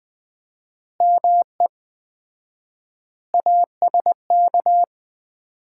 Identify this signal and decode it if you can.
Morse code, heard as a keyed tone, about 20 wpm, 700 Hz. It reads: ME ASK
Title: 4x Word-Spacing - Sets of 4 Words - 20wpm